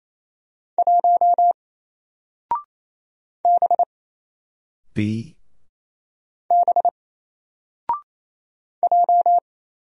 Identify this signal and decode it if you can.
Morse code, heard as a keyed tone, about 27 wpm, 700 Hz. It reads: 1 B B J
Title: Single Letter-Number 28wpm